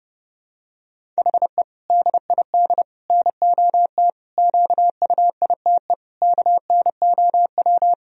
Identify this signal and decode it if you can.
Morse code, heard as a keyed tone, about 30 wpm, 700 Hz. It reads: HE DID NOT QUITE KNOW
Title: Sentences from Top 1000 Words - Review Part 1 - 30wpm